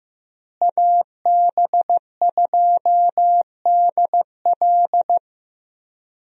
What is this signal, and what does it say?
Morse code, heard as a keyed tone, about 16 wpm, 700 Hz. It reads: AB2DL
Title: USA Call Signs - 5 Characters - Repeated 2x - 15wpm